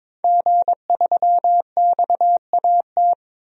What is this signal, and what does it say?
Morse code, heard as a keyed tone, about 22 wpm, 700 Hz. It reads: G3XAT
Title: DXCC-100 Call Signs - 5 Characters - 22wpm